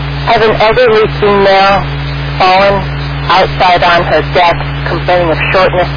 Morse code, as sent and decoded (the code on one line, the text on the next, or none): none